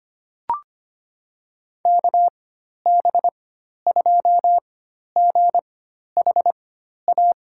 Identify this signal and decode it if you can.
Morse code, heard as a keyed tone, about 25 wpm, spaced as for 12 wpm, 700 Hz. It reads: KB2GHA
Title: USA Call Signs - 6 Characters - 4x Character Spacing - 25wpm